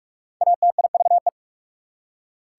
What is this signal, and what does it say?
Morse code, heard as a keyed tone, about 56 wpm, 690 Hz. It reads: ATIVE
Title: Speed-Racing Prefixes and Suffixes 45wpm